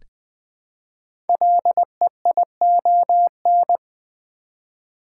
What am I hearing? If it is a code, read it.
Morse code, heard as a keyed tone, about 20 wpm, 700 Hz. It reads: LEION
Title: Code Groups - 5 Letter:Numbers - 20wpm